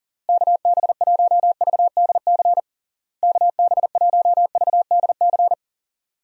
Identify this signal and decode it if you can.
Morse code, heard as a keyed tone, about 37 wpm, 700 Hz. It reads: KB1VDC KB1VDC
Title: USA Call Signs - 6 Characters - Repeated 3x - 40wpm